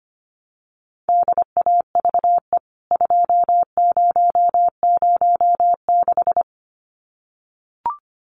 Morse code, heard as a keyed tone, about 25 wpm, 700 Hz. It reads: DAVE 2006
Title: CWT Exchange 25wpm